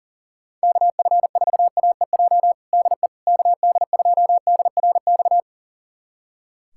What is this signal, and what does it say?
Morse code, heard as a keyed tone, about 40 wpm, 700 Hz. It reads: KF4AEJ DE KD2DRX
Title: USA Callsign DE Callsign - 6 Characters - 40wpm